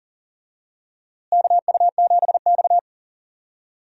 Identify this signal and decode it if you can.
Morse code, heard as a keyed tone, about 40 wpm, 700 Hz. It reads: KU7X
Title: USA Call Signs - 4 Characters - Repeated 2x - 40wpm